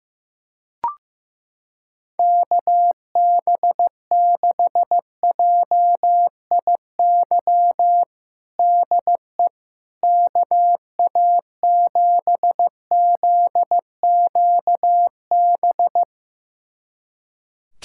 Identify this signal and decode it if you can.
Morse code, heard as a keyed tone, about 15 wpm, 700 Hz. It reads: KB6JIY DE KA7ZQB
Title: USA Callsign DE Callsign - 6 Characters - 15wpm